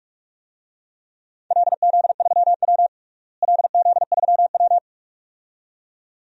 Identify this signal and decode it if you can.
Morse code, heard as a keyed tone, about 46 wpm, 700 Hz. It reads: LZ3W LZ3W
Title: DXCC-100 Call Signs - 4 Characters - Repeated 2x - 45wpm